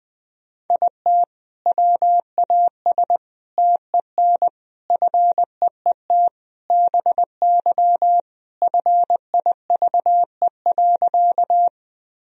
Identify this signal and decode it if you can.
Morse code, heard as a keyed tone, about 20 wpm, 700 Hz. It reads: IT WAS TEN FEET BY FIVE.
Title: Sentences from Top 1000 Words - Review Part 2 - 20wpm